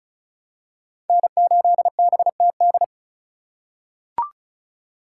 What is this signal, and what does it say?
Morse code, heard as a keyed tone, about 36 wpm, 700 Hz. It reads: N8BTD
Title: USA Call Signs - 5 Characters - 4x Character Spacing - 35wpm